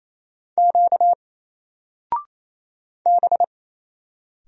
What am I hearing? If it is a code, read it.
Morse code, heard as a keyed tone, about 27 wpm, 700 Hz. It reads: Q B
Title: Single Character (Letter:Number:Punctuation:Prosign) 28wpm